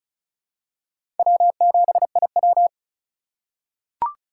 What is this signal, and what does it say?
Morse code, heard as a keyed tone, about 34 wpm, 700 Hz. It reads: W7IW
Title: USA Call Signs - 4 Characters - 2x Character Spacing - 35wpm